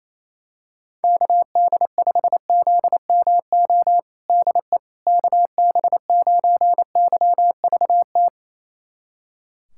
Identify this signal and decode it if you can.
Morse code, heard as a keyed tone, about 27 wpm, 700 Hz. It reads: KD5ZMO DE KB9YVT
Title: USA Callsign DE Callsign - 6 Characters - 28wpm